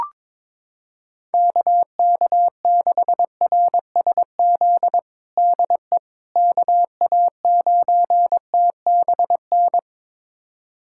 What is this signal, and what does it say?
Morse code, heard as a keyed tone, about 22 wpm, 700 Hz. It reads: KK6RSZ DE KA9TBN
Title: USA Callsign DE Callsign 22wpm